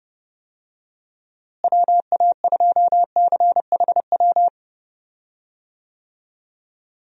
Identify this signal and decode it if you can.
Morse code, heard as a keyed tone, about 31 wpm, 700 Hz. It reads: WA2CHW
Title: USA Call Signs - 6 Characters - Repeated 3x - 30wpm